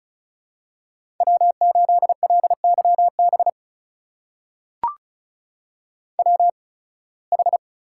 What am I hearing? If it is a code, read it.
Morse code, heard as a keyed tone, about 36 wpm, spaced as for 23 wpm, 700 Hz. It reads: W8LYB WH
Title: USA Call Signs - 5 Characters - 8x Character Spacing - 35wpm